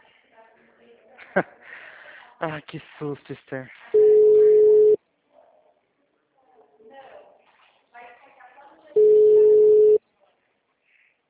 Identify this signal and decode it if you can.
Ringback tone (Europe, 425 Hz), with speech.